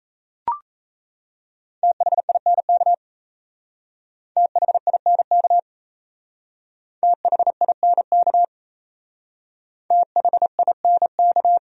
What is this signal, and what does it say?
Morse code, heard as a keyed tone, about 34 wpm, 700 Hz. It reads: THINK THINK THINK THINK
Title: Speed-Racing Top 500 Words 28wpm